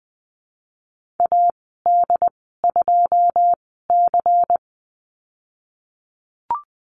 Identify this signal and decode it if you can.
Morse code, heard as a keyed tone, about 20 wpm, spaced as for 15 wpm, 700 Hz. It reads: AD2C
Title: 128 Lesson 21g - Number 6 - Call Signs 20wpm